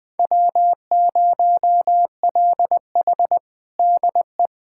Morse code, heard as a keyed tone, about 20 wpm, 700 Hz. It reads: W0LH DE
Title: USA Callsign DE Callsign 20wpm